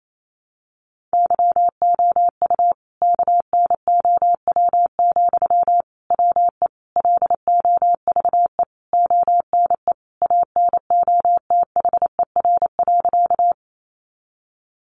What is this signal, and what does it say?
Morse code, heard as a keyed tone, about 28 wpm, 700 Hz. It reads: YOU KNOW, WE LOVE ONE ANOTHER.
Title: Sentences from Top 600 Words 28wpm